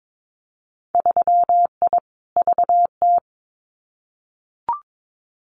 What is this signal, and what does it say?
Morse code, heard as a keyed tone, about 22 wpm, 700 Hz. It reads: 3I VT
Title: Winter Field Day Exchanges Encore 22wpm